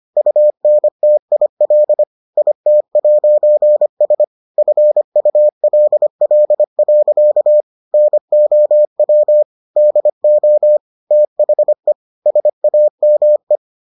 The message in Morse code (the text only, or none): UNTIL IT'S FULL. NOW DO THE SAME